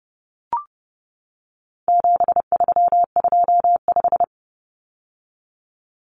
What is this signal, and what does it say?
Morse code, heard as a keyed tone, about 30 wpm, 700 Hz. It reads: 7325
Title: Numbers - 4-Digit 30wpm